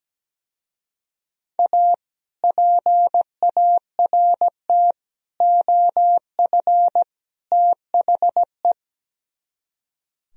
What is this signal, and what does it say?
Morse code, heard as a keyed tone, about 17 wpm, 700 Hz. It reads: A PART OF THE
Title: Speed-Racing - Sets of 4 Words 17wpm